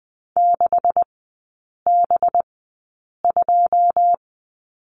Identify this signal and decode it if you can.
Morse code, heard as a keyed tone, about 20 wpm, spaced as for 8 wpm, 700 Hz. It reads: 6B2
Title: 184 Lesson 30e - Letter b - Three Characters 20wpm